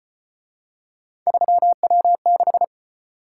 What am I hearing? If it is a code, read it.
Morse code, heard as a keyed tone, about 34 wpm, 700 Hz. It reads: 3W6